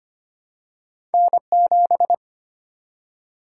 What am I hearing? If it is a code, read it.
Morse code, heard as a keyed tone, about 25 wpm, 700 Hz. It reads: N 7
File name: Two Letter-Number Combinations - 25wpm